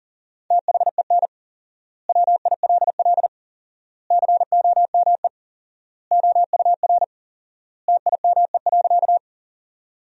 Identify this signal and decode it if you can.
Morse code, heard as a keyed tone, about 37 wpm, 700 Hz. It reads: THEN WILL COME OUR TIME.
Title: 4x Word-Spacing - Sentences from Top 200 Words - 40wpm